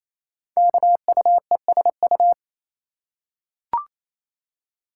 Morse code, heard as a keyed tone, about 28 wpm, 700 Hz. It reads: KUESU
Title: Code Groups - 5 Letter:Numbers - 28wpm